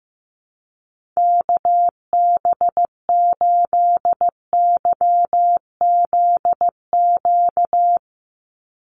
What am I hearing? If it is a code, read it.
Morse code, heard as a keyed tone, about 15 wpm, 700 Hz. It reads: KB8YZQ
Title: USA Call Signs 15wpm